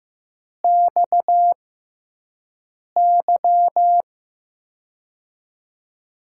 Morse code, heard as a keyed tone, about 15 wpm, spaced as for 5 wpm, 700 Hz. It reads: XY
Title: USA Call Signs - 5 Characters - 6x Character Spacing - 15wpm